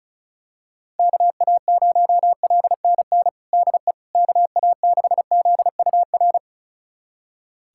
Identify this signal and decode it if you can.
Morse code, heard as a keyed tone, about 36 wpm, 700 Hz. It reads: KA0LNN DE KA6ZUR
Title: USA Callsign DE Callsign - 6 Characters - 35wpm